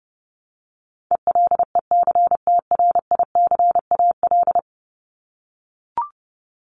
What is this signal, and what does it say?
Morse code, heard as a keyed tone, about 29 wpm, 700 Hz. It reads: ELECTRICAL
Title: Top 751-1000 Words 30wpm